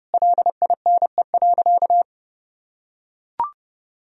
Morse code, heard as a keyed tone, about 30 wpm, 700 Hz. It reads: LINE.
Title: Sentences from Top 1000 Words - Review Part 3 - 30wpm